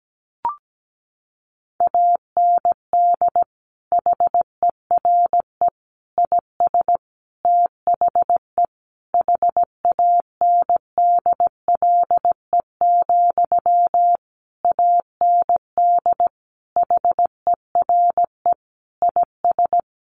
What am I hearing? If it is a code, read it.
Morse code, heard as a keyed tone, about 17 wpm, 700 Hz. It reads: AND HERE IS THE HANDLE, AND HERE IS
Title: Sentences from Top 1100 Words 17wpm